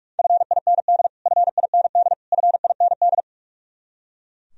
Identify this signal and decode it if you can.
Morse code, heard as a keyed tone, about 41 wpm, 700 Hz. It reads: FIND FIND FIND
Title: Top 100 Words Repeated 45wpm